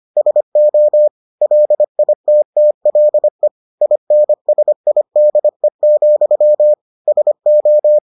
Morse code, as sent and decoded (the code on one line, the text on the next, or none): ... --- / .-.. .. - - .-.. / .. -. ... .. -.. --..-- / ... ---
SO LITTLE INSIDE, SO